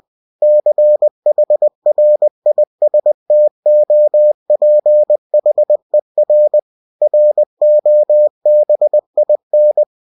Morse code, heard as a keyed tone, about 20 wpm, 600 Hz. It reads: CHRISTOPHER ROBIN